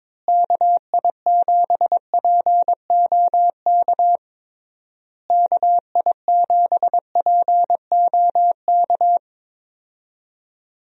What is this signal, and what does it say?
Morse code, heard as a keyed tone, about 22 wpm, 700 Hz. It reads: KI7POK KI7POK
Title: USA Call Signs - 6 Characters - Repeated 3x - 22wpm